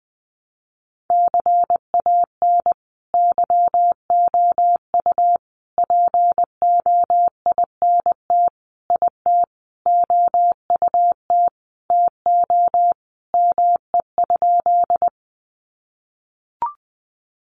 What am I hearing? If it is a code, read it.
Morse code, heard as a keyed tone, about 20 wpm, 700 Hz. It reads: CAN YOU POINT IT OUT TO ME?
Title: Sentences from Top 200 Words 20wpm